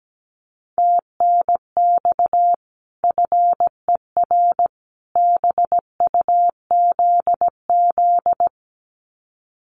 Morse code, heard as a keyed tone, about 17 wpm, 700 Hz. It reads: TNX FER BUZZ
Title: Common QSO Phrases 17wpm